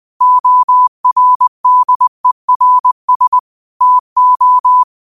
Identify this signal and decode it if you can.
Morse code, heard as a keyed tone, about 20 wpm, 1000 Hz. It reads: ORDERS TO
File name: qotdfiles / 20wpm